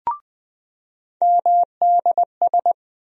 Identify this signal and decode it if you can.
Morse code, heard as a keyed tone, about 20 wpm, 700 Hz. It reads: MDS
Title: USA Call Sign Postfixes - 3 Letters 20wpm